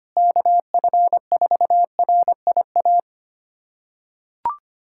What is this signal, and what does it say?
Morse code, heard as a keyed tone, about 25 wpm, 700 Hz. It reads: KF4RIA
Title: USA Callsign DE Callsign 25wpm